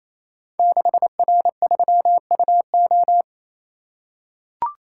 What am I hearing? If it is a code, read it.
Morse code, heard as a keyed tone, about 27 wpm, 700 Hz. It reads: 6R3UO
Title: Code Groups - 5 Letter:Numbers - 28wpm